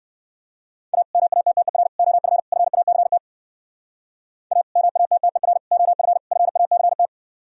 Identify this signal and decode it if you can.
Morse code, heard as a keyed tone, about 59 wpm, 700 Hz. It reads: A MATTER OF FACT A MATTER OF FACT